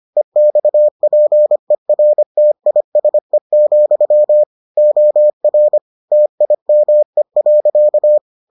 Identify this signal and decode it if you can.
Morse code, heard as a keyed tone, about 25 wpm, 600 Hz. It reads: EXPERTISE, OR TIME.